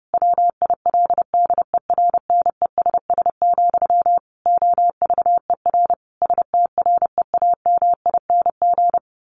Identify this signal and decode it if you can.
Morse code, heard as a keyed tone, about 30 wpm, 700 Hz. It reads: WILDERNESS, OVER STREAMING
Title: A Sound Of Thunder 30wpm